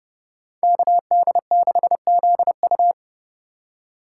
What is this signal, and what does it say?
Morse code, heard as a keyed tone, about 34 wpm, 700 Hz. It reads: KD6ZU
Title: Speed Racing - USA 2x2 Call Signs 30wpm